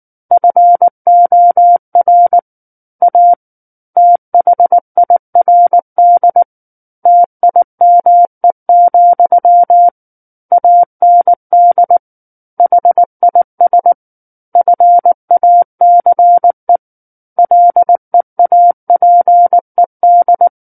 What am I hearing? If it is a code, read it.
Morse code, heard as a keyed tone, about 19 wpm, 700 Hz. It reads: FOR A THIRD TIME, AND HIS FACE LEAPED
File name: Book One / 10-Chapter 10 - 19 WPM